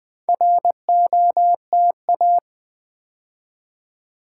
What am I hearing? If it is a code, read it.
Morse code, heard as a keyed tone, about 20 wpm, 690 Hz. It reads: ROTA
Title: Build A Word - 2 Letters At A Time - 6-Letter Words - 20wpm